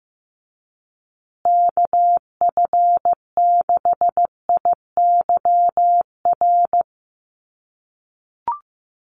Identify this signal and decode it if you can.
Morse code, heard as a keyed tone, about 14 wpm, 700 Hz. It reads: KF6IYR
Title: USA Call Signs Repeated 3x 15wpm